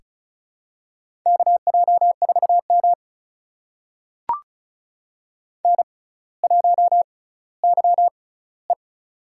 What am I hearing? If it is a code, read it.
Morse code, heard as a keyed tone, about 36 wpm, spaced as for 18 wpm, 700 Hz. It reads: KJ4M N1YE
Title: USA Call Signs - 4 Characters - 6x Character Spacing - 35wpm